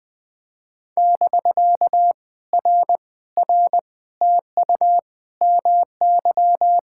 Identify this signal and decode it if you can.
Morse code, heard as a keyed tone, about 19 wpm, 700 Hz. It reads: <BK> R R TU MY
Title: POTA QSO Practice - 20wpm